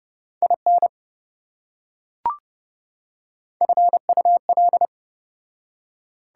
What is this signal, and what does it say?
Morse code, heard as a keyed tone, about 29 wpm, 700 Hz. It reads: IN FUL